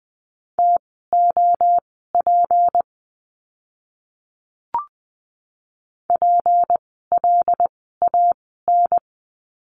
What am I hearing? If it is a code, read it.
Morse code, heard as a keyed tone, about 20 wpm, spaced as for 14 wpm, 700 Hz. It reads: TOP PLAN
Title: 153 Lesson 25f - Letter p - Words 20wpm